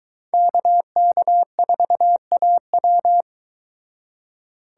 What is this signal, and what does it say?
Morse code, heard as a keyed tone, about 22 wpm, 700 Hz. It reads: KK4AW